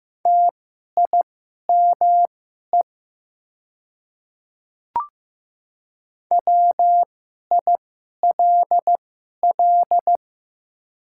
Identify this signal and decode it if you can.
Morse code, heard as a keyed tone, about 15 wpm, spaced as for 11 wpm, 700 Hz. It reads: TIME WILL
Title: 2x Character Spacing - Top 100 Words - 15wpm